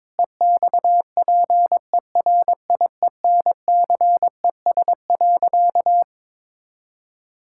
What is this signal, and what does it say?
Morse code, heard as a keyed tone, about 22 wpm, 700 Hz. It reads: EXPERIENCES.
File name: Difficult Sentences with Hints - 22wpm